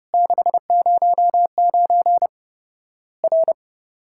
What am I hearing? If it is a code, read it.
Morse code, heard as a keyed tone, about 30 wpm, 700 Hz. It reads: 609 R
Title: Worked All Europe DX Contest - QTC Exchange - 30wpm